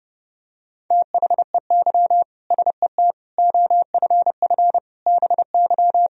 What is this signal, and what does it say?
Morse code, heard as a keyed tone, about 30 wpm, 700 Hz. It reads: THEY SET OFF BY
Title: Sentences from Top 1000 Words - Review Part 3 - 30wpm